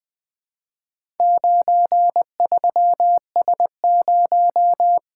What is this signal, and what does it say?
Morse code, heard as a keyed tone, about 20 wpm, 700 Hz. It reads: 93S0
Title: Four Letter-Number Combinations - 20wpm